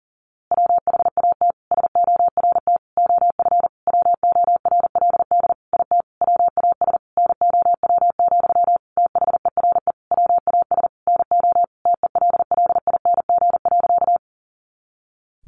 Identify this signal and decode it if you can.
Morse code, heard as a keyed tone, about 37 wpm, 700 Hz. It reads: WHAT SORT OF WORLD IT WAS NOW, THERE WAS NO TELLING.
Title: A Sound Of Thunder 40wpm